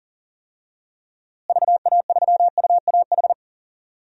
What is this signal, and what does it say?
Morse code, heard as a keyed tone, about 40 wpm, 700 Hz. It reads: VA3UAH
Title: DXCC-100 Call Signs - 6 Characters - Repeated 3x - 40wpm